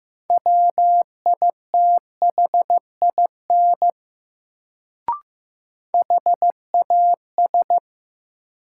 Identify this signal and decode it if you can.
Morse code, heard as a keyed tone, about 15 wpm, 700 Hz. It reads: WITHIN HAS
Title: Sets of 2 Words - New Word Drill - 15wpm